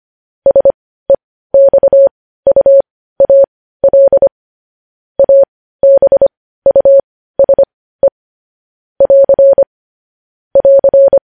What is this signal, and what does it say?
Morse code, heard as a keyed tone, about 25 wpm, spaced as for 15 wpm, 550 Hz. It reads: SEXUAL ABUSE <AR> <AR>
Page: news-headlines-in-morse-code-at-15-wpm